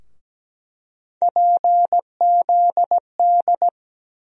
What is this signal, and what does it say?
Morse code, heard as a keyed tone, about 17 wpm, 700 Hz. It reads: PZD
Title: USA Call Sign Postfixes - 3 Letters 17wpm